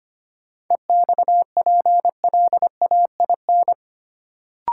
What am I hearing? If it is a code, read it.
Morse code, heard as a keyed tone, about 25 wpm, 700 Hz. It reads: EXPLAIN
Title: Sentences from Top 1100 Words - New Word Drill - 25wpm